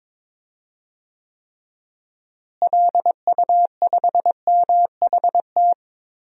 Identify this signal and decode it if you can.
Morse code, heard as a keyed tone, about 22 wpm, 700 Hz. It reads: LU5MHT
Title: DXCC-100 Call Signs - 6 Characters - Repeated 2x - 22wpm